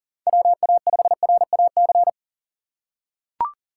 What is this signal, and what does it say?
Morse code, heard as a keyed tone, about 37 wpm, 700 Hz. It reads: WA5RAC
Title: USA Call Signs - 4x Character Spacing - 40wpm